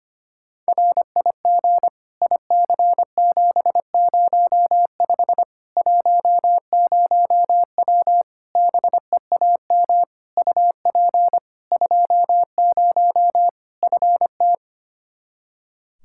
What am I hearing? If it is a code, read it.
Morse code, heard as a keyed tone, about 25 wpm, 700 Hz. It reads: RIG IC705 10W BEAM UP 20 FT
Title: Practice QSO - Exchange 2 - Info - 25wpm